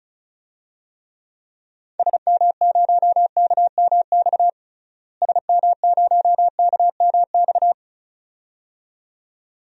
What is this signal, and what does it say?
Morse code, heard as a keyed tone, about 36 wpm, 700 Hz. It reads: SM0KMX SM0KMX
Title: DXCC-100 Call Signs - 6 Characters - Repeated 2x - 35wpm